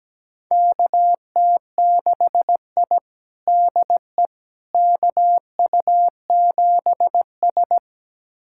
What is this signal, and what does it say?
Morse code, heard as a keyed tone, about 17 wpm, 700 Hz. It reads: KT6I DE KU7S
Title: USA Callsign DE Callsign - 4 Characters - 17wpm